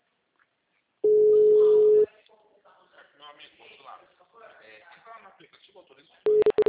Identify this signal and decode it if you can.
Ringback tone (Europe, 425 Hz), with speech.
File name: out- -2006- - - .58264